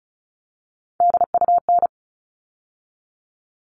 Morse code, heard as a keyed tone, about 34 wpm, 700 Hz. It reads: DUN
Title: All Cut Numbers - 3-Digit 35wpm